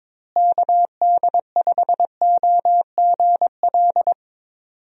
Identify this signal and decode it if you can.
Morse code, heard as a keyed tone, about 22 wpm, 700 Hz. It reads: KD5OGL
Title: USA Call Signs - 6 Characters - Repeated 3x - 22wpm